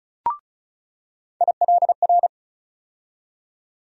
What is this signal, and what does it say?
Morse code, heard as a keyed tone, about 34 wpm, 700 Hz. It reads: ILR
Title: USA Call Sign Postfixes - 3 Letters 35wpm